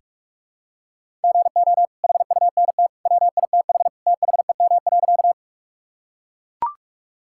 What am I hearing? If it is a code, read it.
Morse code, heard as a keyed tone, about 42 wpm, 700 Hz. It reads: GO HUNT WITH THEM.
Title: Sentences from Top 1600 Words 45wpm